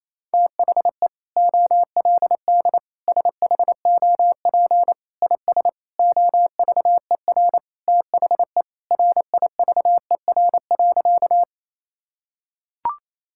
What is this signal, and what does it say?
Morse code, heard as a keyed tone, about 27 wpm, 700 Hz. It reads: THE OLD SHOP IS OVER THE RIVER.
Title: Sentences from Top 1600 Words 28wpm